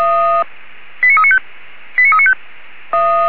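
Five-tone selective call (ZVEI), heard as quick runs of five tones, tones of about 70 ms. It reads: 89395 89395
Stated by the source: ZVEI five-tone call 89406 received by SDR.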